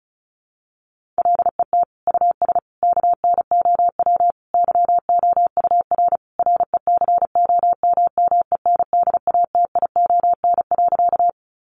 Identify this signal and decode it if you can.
Morse code, heard as a keyed tone, about 34 wpm, 700 Hz. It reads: LET US KNOW YOUR RECOMMENDATION.
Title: Sentences from Top 1400 Words 35wpm